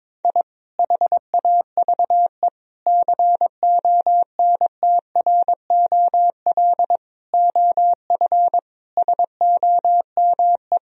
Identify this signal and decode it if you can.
Morse code, heard as a keyed tone, about 22 wpm, 700 Hz. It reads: I HAVE CONTROL OF SOME
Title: Sentences from Top 1000 Words - Review Part 3 - 22wpm